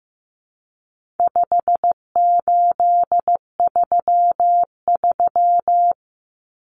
Morse code, heard as a keyed tone, about 15 wpm, 700 Hz. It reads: 5833
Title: Numbers - 4-Digit 15wpm